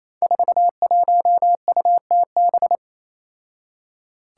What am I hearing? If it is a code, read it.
Morse code, heard as a keyed tone, about 28 wpm, 700 Hz. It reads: 41UTB
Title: Five Challenging Letter-Number Combinations 28wpm